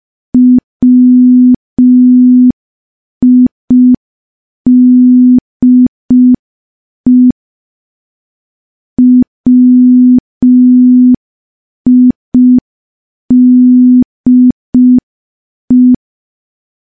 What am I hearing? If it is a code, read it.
Morse code, heard as a keyed tone, about 5 wpm, 250 Hz. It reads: WIDE WIDE
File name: morse / morse-training-05WPM